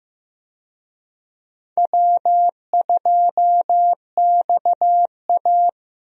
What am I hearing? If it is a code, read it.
Morse code, heard as a keyed tone, about 15 wpm, 700 Hz. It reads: W2XA
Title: USA Call Signs - 4 Characters - Repeated 2x - 15wpm